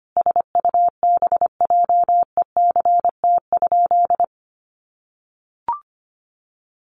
Morse code, heard as a keyed tone, about 25 wpm, 700 Hz. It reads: SUBJECT?
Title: Sentences from Top 1200 Words 25wpm